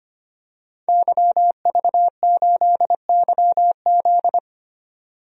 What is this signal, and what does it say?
Morse code, heard as a keyed tone, about 25 wpm, 700 Hz. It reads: YV8YZ
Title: DXCC-100 Call Signs - 5 Characters - 25wpm